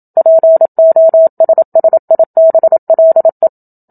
Morse code, heard as a keyed tone, about 27 wpm, 650 Hz. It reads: POSSIBLE
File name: Book Two / 01-Chapter 1 - 27 WPM